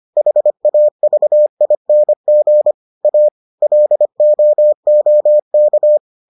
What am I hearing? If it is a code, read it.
Morse code, heard as a keyed tone, about 25 wpm, 600 Hz. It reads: HAVING A LOOK